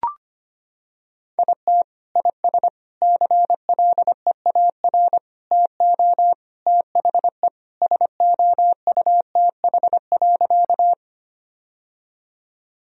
Morse code, heard as a keyed tone, about 25 wpm, 700 Hz. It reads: IT IS CLEAR TO THE SOUTH.
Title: Sentences from Top 1000 Words - Review Part 2 - 25wpm